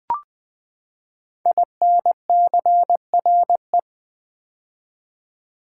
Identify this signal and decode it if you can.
Morse code, heard as a keyed tone, about 20 wpm, 700 Hz. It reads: INCRE
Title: Build A Word - 5 Letters At A Time - 12-Letter Words - 20wpm